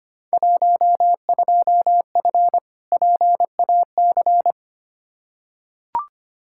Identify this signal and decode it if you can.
Morse code, heard as a keyed tone, about 25 wpm, 700 Hz. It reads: 12F PAC
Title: Field Day Exchanges Encore - 25wpm